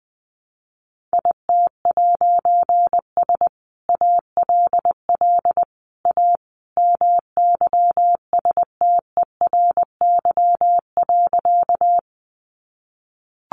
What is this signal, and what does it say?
Morse code, heard as a keyed tone, about 20 wpm, 700 Hz. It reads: IT'S ALL A MYSTERY.
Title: A Sound Of Thunder 20wpm